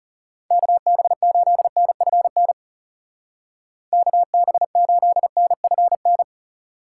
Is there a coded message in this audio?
Yes: Morse code from 0.5 to 6.2 s, heard as a keyed tone, about 40 wpm, 700 Hz. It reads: KB8NFN KB8NFN